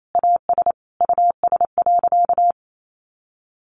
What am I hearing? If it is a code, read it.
Morse code, heard as a keyed tone, about 28 wpm, 700 Hz. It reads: AS US.
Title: Sentences from Top 500 Words 28wpm